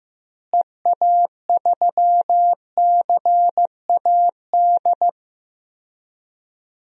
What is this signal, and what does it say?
Morse code, heard as a keyed tone, about 15 wpm, 700 Hz. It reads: EA3CAD
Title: DXCC-100 Call Signs - 6 Characters - Repeated 3x - 15wpm